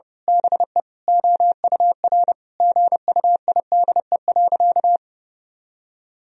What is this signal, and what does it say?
Morse code, heard as a keyed tone, about 31 wpm, 700 Hz. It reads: BE OUR GUIDE.
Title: Sentences from Top 1000 Words - Review Part 3 - 30wpm